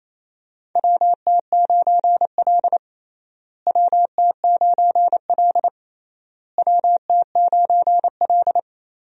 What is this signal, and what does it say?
Morse code, heard as a keyed tone, about 28 wpm, 700 Hz. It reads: WT9L WT9L WT9L
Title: USA Call Signs - 4 Characters - Repeated 3x - 28wpm